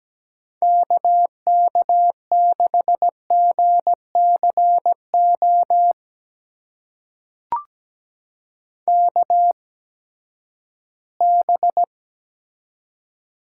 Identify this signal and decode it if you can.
Morse code, heard as a keyed tone, about 17 wpm, spaced as for 12 wpm, 700 Hz. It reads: KK6GCO KB
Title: USA Call Signs - 6 Characters - 8x Character Spacing - 17wpm